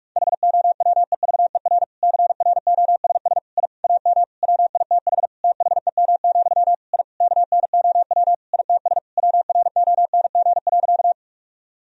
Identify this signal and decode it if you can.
Morse code, heard as a keyed tone, about 46 wpm, 700 Hz. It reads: HOWEVER CROSS I AM WITH THEM, I KNOW ITS WRONG.
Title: Sentences from Top 1000 Words - Review Part 1 - 45wpm